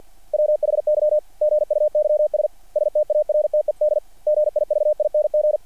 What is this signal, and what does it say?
Morse code, heard as a keyed tone, about 48 wpm, 600 Hz. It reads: WHY GUYS STARTED GIVING
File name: archived / CRSnet